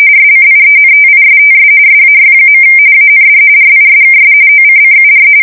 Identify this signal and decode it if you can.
RTTY, about 45.45 baud. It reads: SCIENTISTS SAY. AND RISKS OCCUR
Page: 170 Shift 2125/2295